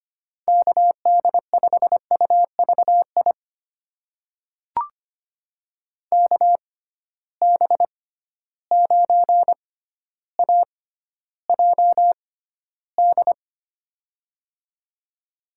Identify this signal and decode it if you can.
Morse code, heard as a keyed tone, about 25 wpm, spaced as for 13 wpm, 700 Hz. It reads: KD5UVI KB9AJD
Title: USA Call Signs - 6 Characters - 6x Character Spacing - 25wpm